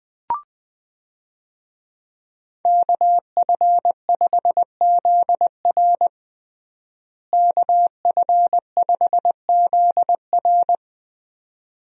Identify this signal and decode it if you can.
Morse code, heard as a keyed tone, about 20 wpm, 700 Hz. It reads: KF5ZR KF5ZR
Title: USA Call Signs - 5 Characters - Repeated 2x - 20wpm